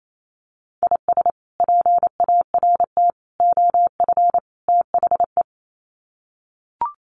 Morse code, heard as a keyed tone, about 28 wpm, 700 Hz. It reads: IS PART OF THE
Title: Sets of 4 Words 28wpm